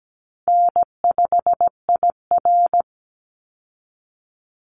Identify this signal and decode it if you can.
Morse code, heard as a keyed tone, about 17 wpm, 700 Hz. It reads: N5IR
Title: Call Signs Repeated 2x - 17wpm